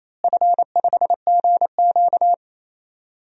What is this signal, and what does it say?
Morse code, heard as a keyed tone, about 28 wpm, 700 Hz. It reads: F5GQ
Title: DXCC-100 Call Signs - 4 Characters - Repeated 3x - 28wpm